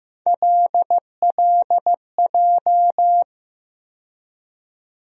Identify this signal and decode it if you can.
Morse code, heard as a keyed tone, about 15 wpm, 700 Hz. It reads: LLJ
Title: USA Call Sign Postfixes - 3 Letters 15wpm